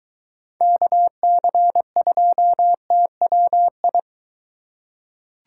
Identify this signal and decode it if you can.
Morse code, heard as a keyed tone, about 23 wpm, 700 Hz. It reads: KC2TWI